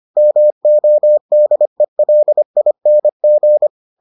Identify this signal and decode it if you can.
Morse code, heard as a keyed tone, about 25 wpm, 600 Hz. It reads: MODELING